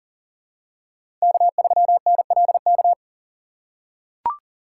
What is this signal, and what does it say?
Morse code, heard as a keyed tone, about 40 wpm, 700 Hz. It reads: K3NLK
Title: USA Call Signs - 6x Character Spacing - 40wpm